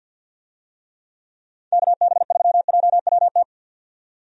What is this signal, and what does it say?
Morse code, heard as a keyed tone, about 51 wpm, 700 Hz. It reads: KB3JWT
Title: USA Call Signs - 6 Characters - 50wpm